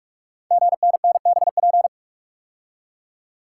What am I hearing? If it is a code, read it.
Morse code, heard as a keyed tone, about 44 wpm, 690 Hz. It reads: GNNBP
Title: Code Groups - 5 Letters - 45wpm